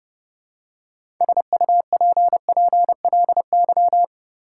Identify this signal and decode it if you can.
Morse code, heard as a keyed tone, about 29 wpm, 700 Hz. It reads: SUPPLY
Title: Six Letter Words - 30wpm